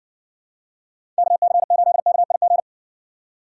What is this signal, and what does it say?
Morse code, heard as a keyed tone, about 62 wpm, 700 Hz. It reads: KC8KIG